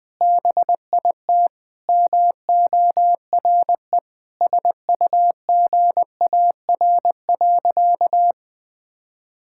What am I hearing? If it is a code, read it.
Morse code, heard as a keyed tone, about 21 wpm, 700 Hz. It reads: BIT MORE SUGAR.
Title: Sentences with 4-Word Phrases - 20wpm